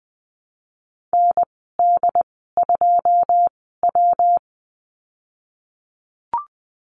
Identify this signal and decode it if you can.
Morse code, heard as a keyed tone, about 20 wpm, spaced as for 15 wpm, 700 Hz. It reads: ND2W
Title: 179 Lesson 29g - Slash - Call Signs 20wpm